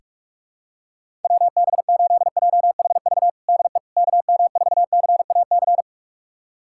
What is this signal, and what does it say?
Morse code, heard as a keyed tone, about 46 wpm, 700 Hz. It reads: WB8JHV DE KM4CAC
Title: USA Callsign DE Callsign 45wpm